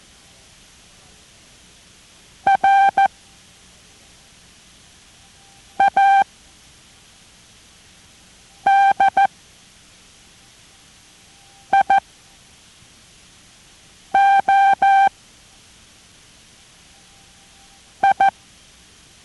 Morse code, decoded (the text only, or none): RADIO I